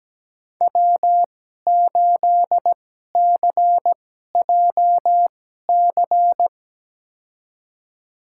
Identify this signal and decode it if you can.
Morse code, heard as a keyed tone, about 17 wpm, spaced as for 12 wpm, 700 Hz. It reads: W8CJC
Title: USA Call Signs - 5 Characters - 2x Character Spacing - 17wpm